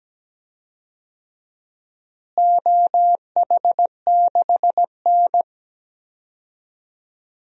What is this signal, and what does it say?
Morse code, heard as a keyed tone, about 17 wpm, 700 Hz. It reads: OH6N
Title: DXCC-100 Call Signs - 4 Characters - Repeated 3x - 17wpm